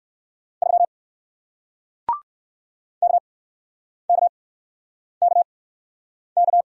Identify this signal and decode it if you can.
Morse code, heard as a keyed tone, about 60 wpm, spaced as for 9 wpm, 700 Hz. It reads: 3 KKKK